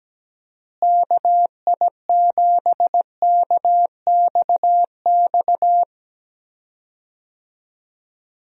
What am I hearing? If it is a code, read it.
Morse code, heard as a keyed tone, about 17 wpm, 700 Hz. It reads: KI7KXX
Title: USA Call Signs - 6 Characters - Repeated 3x - 17wpm